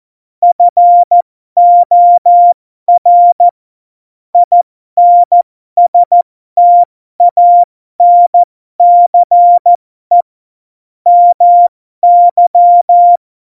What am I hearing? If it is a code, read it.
Morse code, heard as a keyed tone, about 14 wpm, spaced as for 12 wpm, 700 Hz. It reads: FOR INSTANCE MY
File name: Morsekurs / Witze